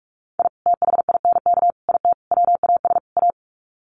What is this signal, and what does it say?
Morse code, heard as a keyed tone, about 44 wpm, 700 Hz. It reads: I THINK IT WAS A
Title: Sets of 5 Words 45wpm